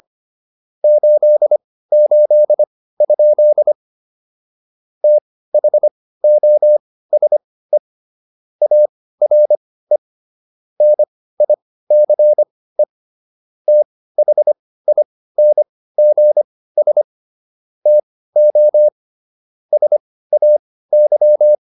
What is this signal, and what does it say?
Morse code, heard as a keyed tone, about 25 wpm, spaced as for 16 wpm, 600 Hz. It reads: 88? THOSE ARE NICE THINGS TO SAY